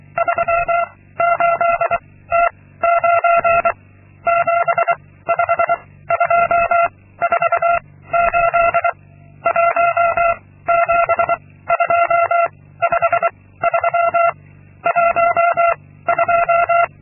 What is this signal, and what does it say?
Morse code, heard as a keyed tone, about 23 wpm, 680 Hz. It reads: 3 8 T 9 7 5 2 4 8 1 7 2 5 3 1 2
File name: Linksmakalnis-2